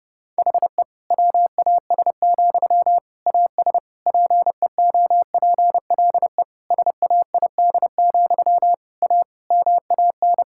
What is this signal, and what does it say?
Morse code, heard as a keyed tone, about 30 wpm, 700 Hz. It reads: HE WAS, AS PEOPLE SAID, A MAN
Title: Sentences from Top 1000 Words - Review Part 6 - 30wpm